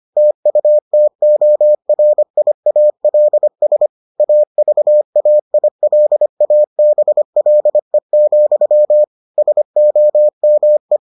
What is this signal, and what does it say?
Morse code, heard as a keyed tone, about 25 wpm, 600 Hz. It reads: TUTORIALS AVAILABLE, SOME